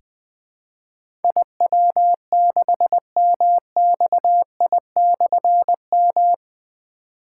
Morse code, heard as a keyed tone, about 20 wpm, 700 Hz. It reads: IW6MXI/M
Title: Difficult Call Signs 20wpm